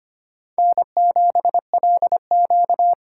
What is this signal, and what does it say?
Morse code, heard as a keyed tone, about 25 wpm, 700 Hz. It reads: N7LQ
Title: USA Call Signs - 4 Characters - 4x Character Spacing - 25wpm